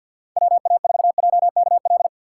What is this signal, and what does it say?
Morse code, heard as a keyed tone, about 47 wpm, 700 Hz. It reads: WA4JKL
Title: K1USN SST QSO Practice - 50wpm